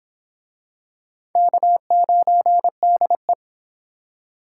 Morse code, heard as a keyed tone, about 24 wpm, 700 Hz. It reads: K9DE